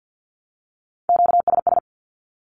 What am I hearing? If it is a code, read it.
Morse code, heard as a keyed tone, about 48 wpm, 700 Hz. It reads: QSS
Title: Q-Codes 50wpm